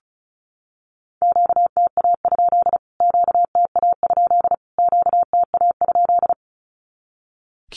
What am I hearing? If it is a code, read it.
Morse code, heard as a keyed tone, about 34 wpm, 700 Hz. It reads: QTA? QTA? QTA?
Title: Q-Codes Repeated 3x 35wpm